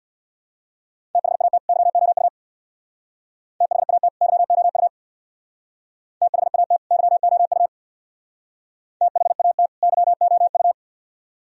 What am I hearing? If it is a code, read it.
Morse code, heard as a keyed tone, about 60 wpm, 700 Hz. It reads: THAT YOU THAT YOU THAT YOU THAT YOU
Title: Speed-Racing - Sets of 2 Words 50wpm